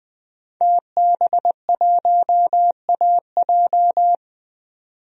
Morse code, heard as a keyed tone, about 20 wpm, 700 Hz. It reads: TB1AJ
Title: DXCC-100 Call Signs - 5 Characters - 20wpm